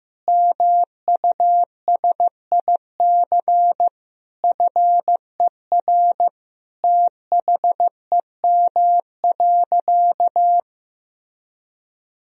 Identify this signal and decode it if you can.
Morse code, heard as a keyed tone, about 15 wpm, 700 Hz. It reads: MUSIC FER THEM.
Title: Sentences from Top 300 Words and Abbreviations 15wpm